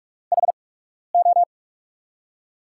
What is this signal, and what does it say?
Morse code, heard as a keyed tone, about 45 wpm, spaced as for 12 wpm, 700 Hz. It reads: HO
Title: USA Call Signs - 6 Characters - 8x Character Spacing - 45wpm